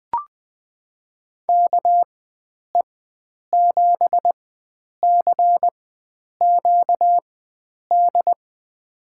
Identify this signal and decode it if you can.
Morse code, heard as a keyed tone, about 20 wpm, spaced as for 9 wpm, 700 Hz. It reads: KE7CQD
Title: USA Call Signs - 6 Characters - 4x Character Spacing - 20wpm